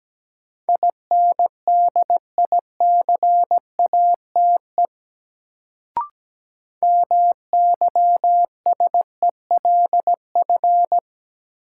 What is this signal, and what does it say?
Morse code, heard as a keyed tone, about 17 wpm, 700 Hz. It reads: INDICATE MYSELF
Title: Sentences from Top 1300 Words - New Word Drill - 17wpm